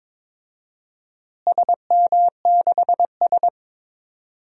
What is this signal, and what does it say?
Morse code, heard as a keyed tone, about 23 wpm, 700 Hz. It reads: SM6S
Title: DXCC-100 Call Signs - 4 Characters - Repeated 2x - 22wpm